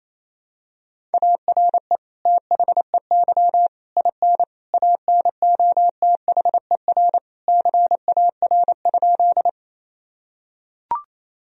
Morse code, heard as a keyed tone, about 28 wpm, 700 Hz. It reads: ARE THEY IN ANOTHER CAR?
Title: Sentences from Top 1000 Words - Review Part 1 - 28wpm